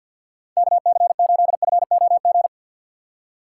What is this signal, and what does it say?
Morse code, heard as a keyed tone, about 50 wpm, 700 Hz. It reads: KC8FOG
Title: USA Call Signs - 4x Character Spacing - 50wpm